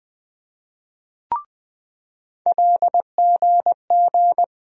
Morse code, heard as a keyed tone, about 20 wpm, 700 Hz. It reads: LGG
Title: USA Call Sign Postfixes - 3 Letters 20wpm